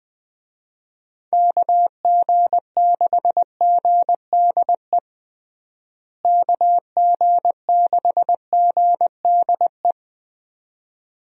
Morse code, heard as a keyed tone, about 20 wpm, 700 Hz. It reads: KG6GDE KG6GDE
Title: USA Call Signs - 6 Characters - Repeated 2x - 20wpm